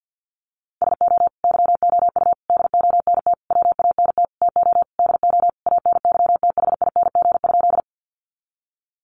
Morse code, heard as a keyed tone, about 50 wpm, 700 Hz. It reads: SO YOU DONT WANT TO DO ANYTHING?
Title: Sentences from Top 1000 Words 50wpm